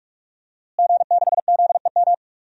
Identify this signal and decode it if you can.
Morse code, heard as a keyed tone, about 46 wpm, 700 Hz. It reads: G6ZEM
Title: DXCC-100 Call Signs - 5 Characters - Repeated 3x - 45wpm